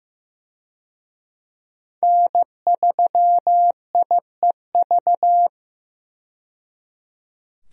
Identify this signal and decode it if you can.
Morse code, heard as a keyed tone, about 15 wpm, 700 Hz. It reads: N3IEV
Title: USA Call Signs - 5 Characters - 15wpm